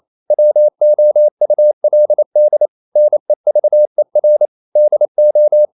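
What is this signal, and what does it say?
Morse code, heard as a keyed tone, about 28 wpm, 600 Hz. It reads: WOULD NEVER DO
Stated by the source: Morse code practice at 28 wpm, text of Ken Follett's novel Never in English, from CW med Gnister.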